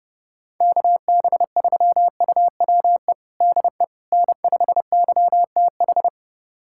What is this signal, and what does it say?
Morse code, heard as a keyed tone, about 30 wpm, 700 Hz. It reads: KB3UWE DE N5YTH
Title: USA Callsign DE Callsign 30wpm